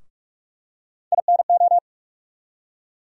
Morse code, heard as a keyed tone, about 47 wpm, 700 Hz. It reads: INO
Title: USA Call Sign Postfixes - 3 Letters 45wpm